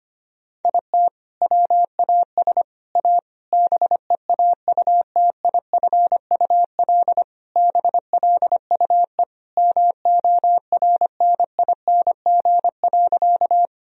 Morse code, heard as a keyed tone, about 25 wpm, 700 Hz. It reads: IT WAS A BEAUTIFUL BLUE MORNING.
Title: Sentences from Top 1000 Words - Review Part 3 - 25wpm